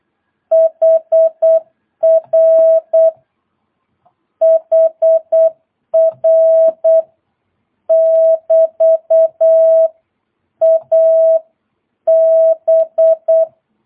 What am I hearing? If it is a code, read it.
Morse code, heard as a keyed tone, about 8 wpm, 650 Hz. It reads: HR HR <BT> AB
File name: jh8cbh / 2022cw